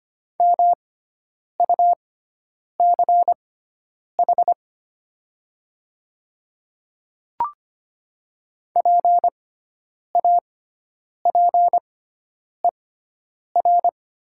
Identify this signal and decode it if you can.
Morse code, heard as a keyed tone, about 25 wpm, spaced as for 9 wpm, 700 Hz. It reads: MUCH PAPER
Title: 6x Character Spacing - Top 500 Words - 25wpm